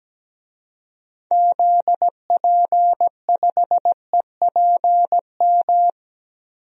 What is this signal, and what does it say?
Morse code, heard as a keyed tone, about 17 wpm, 700 Hz. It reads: ZP5EPM
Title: DXCC-100 Call Signs - 6 Characters - 17wpm